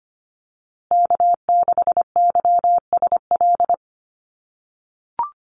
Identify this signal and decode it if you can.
Morse code, heard as a keyed tone, about 25 wpm, 700 Hz. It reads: K6YSL
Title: USA Call Signs Repeated 2x 25wpm